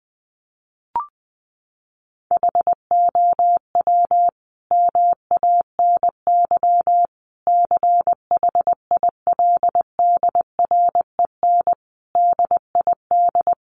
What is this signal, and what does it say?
Morse code, heard as a keyed tone, about 20 wpm, 700 Hz. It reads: HOW MANY CHILDREN DID
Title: Sentences from Top 400 Words 20wpm